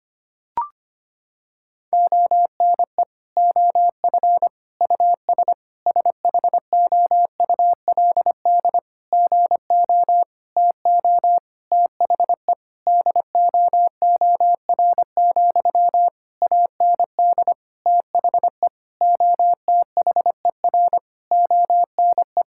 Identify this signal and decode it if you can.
Morse code, heard as a keyed tone, about 25 wpm, 700 Hz. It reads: ONE OF US SHOULD GO TO THE DOOR, AND THE OTHER ONE
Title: Sentences from Top 1600 Words 25wpm